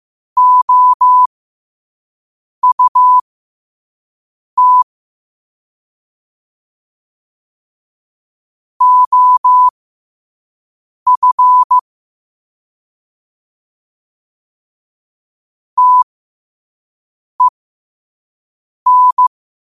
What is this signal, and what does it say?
Morse code, heard as a keyed tone, about 15 wpm, spaced as for 5 wpm, 1000 Hz. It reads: OUT OF TEN
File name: qotdfiles / 5wpm